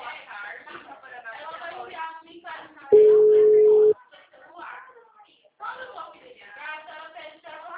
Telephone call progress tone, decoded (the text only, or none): ringback tone